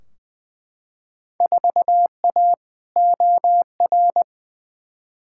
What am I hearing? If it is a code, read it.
Morse code, heard as a keyed tone, about 20 wpm, 700 Hz. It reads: 4A OR
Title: Field Day Exchanges - 20wpm